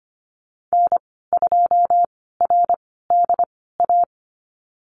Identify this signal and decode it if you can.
Morse code, heard as a keyed tone, about 25 wpm, spaced as for 16 wpm, 700 Hz. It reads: N2RDA
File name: Lesson 15g - Number 5 - Call Signs 25wpm